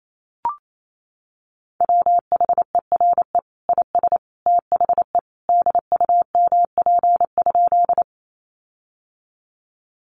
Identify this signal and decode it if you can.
Morse code, heard as a keyed tone, about 28 wpm, 700 Hz. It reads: WHERE IS THE DUMP?
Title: Sentences from Top 1300 Words 28wpm